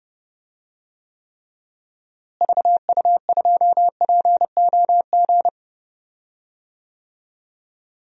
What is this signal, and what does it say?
Morse code, heard as a keyed tone, about 32 wpm, 700 Hz. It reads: VU2POG
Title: DXCC-100 Call Signs - 6 Characters - Repeated 3x - 30wpm